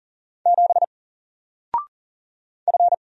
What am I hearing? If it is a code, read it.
Morse code, heard as a keyed tone, about 40 wpm, 700 Hz. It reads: 7 F
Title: Single Letter-Number 40wpm